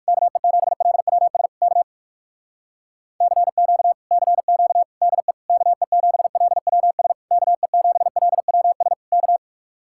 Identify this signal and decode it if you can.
Morse code, heard as a keyed tone, about 59 wpm, 700 Hz. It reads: KE7LWS K CQ CQ DE KE7LWS KE7LWS K